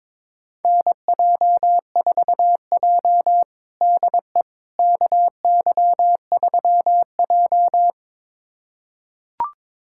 Morse code, heard as a keyed tone, about 22 wpm, 700 Hz. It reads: NJ4J DE KY3J
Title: USA Callsign DE Callsign - 4 Characters - 22wpm